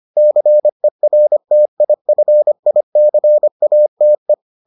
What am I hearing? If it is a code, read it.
Morse code, heard as a keyed tone, about 25 wpm, 600 Hz. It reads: CERTIFICATE